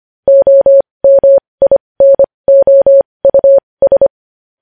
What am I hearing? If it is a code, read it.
Morse code, heard as a keyed tone, about 25 wpm, spaced as for 20 wpm, 550 Hz. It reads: OMINOUS
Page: News Headlines in Morse Code at 20 WPM